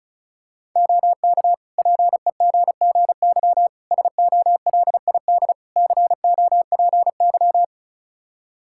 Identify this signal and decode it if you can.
Morse code, heard as a keyed tone, about 36 wpm, 700 Hz. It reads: OK PEGGY SOLID COPY
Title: Practice QSO - Exchange 3 - More Info - 35wpm